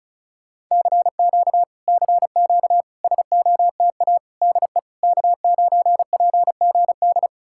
Morse code, heard as a keyed tone, about 40 wpm, 700 Hz. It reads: CQ CQ SOTA DE K9PGD